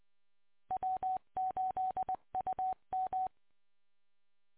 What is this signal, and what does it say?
Morse code, heard as a keyed tone, about 20 wpm, 750 Hz. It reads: W8UM